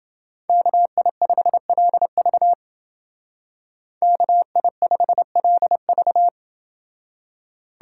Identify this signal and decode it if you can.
Morse code, heard as a keyed tone, about 29 wpm, 700 Hz. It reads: KI5LV KI5LV